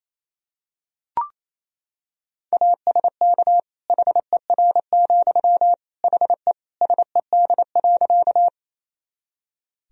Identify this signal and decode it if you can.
Morse code, heard as a keyed tone, about 28 wpm, 700 Hz. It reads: ASK HER, HE SED.
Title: Sentences from Top 200 Words and Abbreviations 28wpm